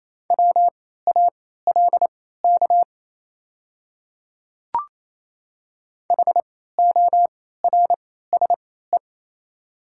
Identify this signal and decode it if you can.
Morse code, heard as a keyed tone, about 28 wpm, spaced as for 16 wpm, 700 Hz. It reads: WALK HORSE
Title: 3x Character Spacing - Top 500 Words - 28wpm